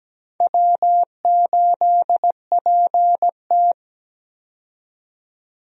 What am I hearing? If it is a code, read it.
Morse code, heard as a keyed tone, about 17 wpm, 700 Hz. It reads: W8PT
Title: USA Call Signs - 4 Characters - 17wpm